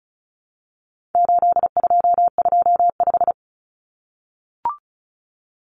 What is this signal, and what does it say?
Morse code, heard as a keyed tone, about 34 wpm, 700 Hz. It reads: 8225
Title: Common Cut Numbers - 4-Digit 35wpm